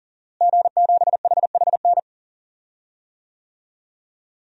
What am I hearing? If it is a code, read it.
Morse code, heard as a keyed tone, about 40 wpm, 700 Hz. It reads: G7HHN
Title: DXCC-100 Call Signs - 5 Characters - Repeated 2x - 40wpm